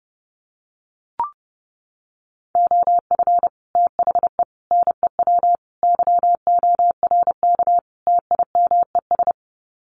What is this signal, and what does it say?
Morse code, heard as a keyed tone, about 30 wpm, 700 Hz. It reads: OF THE NEW YORK TIMES
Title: Sets of 5 Words 30wpm